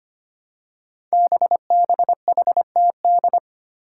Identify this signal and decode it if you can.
Morse code, heard as a keyed tone, about 25 wpm, 700 Hz. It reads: BBHTD
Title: Five Challenging Letter-Number Combinations 25wpm